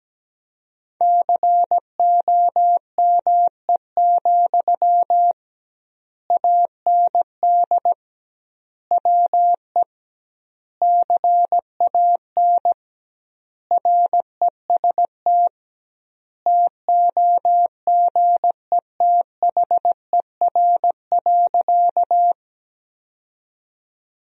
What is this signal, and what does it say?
Morse code, heard as a keyed tone, about 17 wpm, 700 Hz. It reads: COME, AND WE CAN REST TOGETHER.
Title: 2x Word-Spacing - Sentences from Top 500 Words - 17wpm